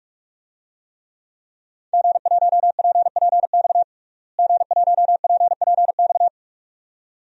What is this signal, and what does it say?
Morse code, heard as a keyed tone, about 41 wpm, 700 Hz. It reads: G1PPX G1PPX
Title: DXCC-100 Call Signs - 5 Characters - Repeated 2x - 45wpm